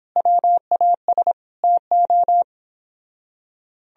Morse code, heard as a keyed tone, about 29 wpm, 700 Hz. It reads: WAS TO